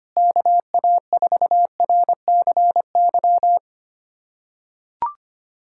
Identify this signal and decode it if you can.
Morse code, heard as a keyed tone, about 25 wpm, 700 Hz. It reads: KA4RCY
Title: USA Callsign DE Callsign 25wpm